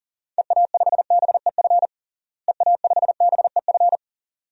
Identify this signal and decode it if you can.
Morse code, heard as a keyed tone, about 37 wpm, 700 Hz. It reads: EA5BEF EA5BEF
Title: DXCC-100 Call Signs - 6 Characters - Repeated 2x - 40wpm